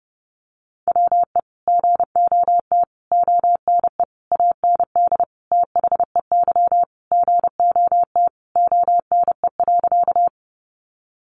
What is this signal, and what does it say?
Morse code, heard as a keyed tone, about 28 wpm, 700 Hz. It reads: WE GOT ONE AND THEY GOT ONE.
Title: Sentences from Top 300 Words 30wpm